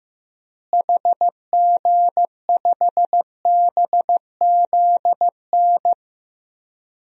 Morse code, heard as a keyed tone, about 16 wpm, 700 Hz. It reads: HG5BZN
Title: DXCC-100 Call Signs - 6 Characters - Repeated 3x - 15wpm